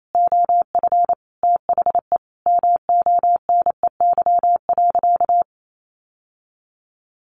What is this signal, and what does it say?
Morse code, heard as a keyed tone, about 28 wpm, 700 Hz. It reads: OF THE MONEY.
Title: Sentences from Top 1100 Words 28wpm